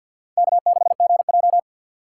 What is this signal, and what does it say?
Morse code, heard as a keyed tone, about 52 wpm, 700 Hz. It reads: K6GJ
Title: USA Call Signs - 2x Character Spacing - 50wpm